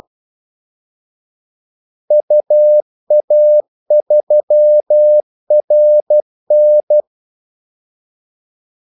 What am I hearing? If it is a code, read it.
Morse code, heard as a keyed tone, about 12 wpm, 600 Hz. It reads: UA3RN
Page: 2 ekstra ordmellemrum mellem kaldesignalerne